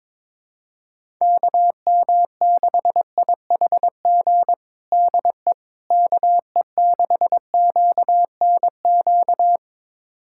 Morse code, heard as a keyed tone, about 22 wpm, 700 Hz. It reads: KM6IHG DE KE6QNQ
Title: USA Callsign DE Callsign - 6 Characters - 22wpm